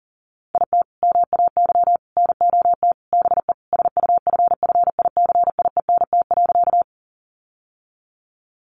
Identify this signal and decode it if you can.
Morse code, heard as a keyed tone, about 40 wpm, 700 Hz. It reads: IT MAY NOT BE SUFFICIENT.
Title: Sentences from Top 1400 Words 40wpm